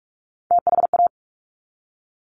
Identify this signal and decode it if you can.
Morse code, heard as a keyed tone, about 46 wpm, 700 Hz. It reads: THA
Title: Top Three Letter Combinations in Words 45wpm